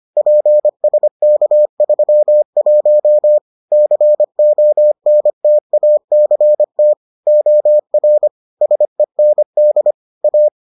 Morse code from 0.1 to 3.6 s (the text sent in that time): PSK31